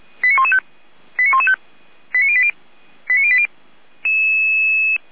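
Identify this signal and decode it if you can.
Five-tone selective call (ZVEI), heard as quick runs of five tones, tones of about 70 ms. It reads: 89115 89115 89080 89080
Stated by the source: ZVEI five-tone selective call received on an SDR.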